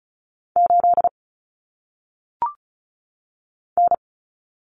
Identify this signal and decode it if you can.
Morse code, heard as a keyed tone, about 35 wpm, 700 Hz. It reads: 8 N
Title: Common Cut Numbers - 1-Digit 35wpm